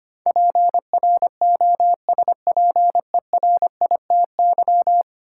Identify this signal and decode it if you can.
Morse code, heard as a keyed tone, about 25 wpm, 700 Hz. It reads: PROSPERITY
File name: Ten Letter Words - 25wpm